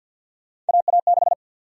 Morse code, heard as a keyed tone, about 46 wpm, 700 Hz. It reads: AA6
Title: USA Call Sign Prefixes - 50wpm